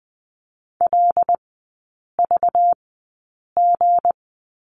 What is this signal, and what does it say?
Morse code, heard as a keyed tone, about 20 wpm, spaced as for 8 wpm, 700 Hz. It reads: LVG
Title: 191 Lesson 31e - Letter v - Three Characters 20wpm